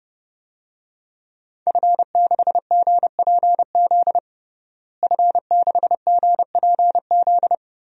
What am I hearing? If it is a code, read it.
Morse code, heard as a keyed tone, about 31 wpm, 700 Hz. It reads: F6GPZ F6GPZ
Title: DXCC-100 Call Signs - 5 Characters - Repeated 3x - 30wpm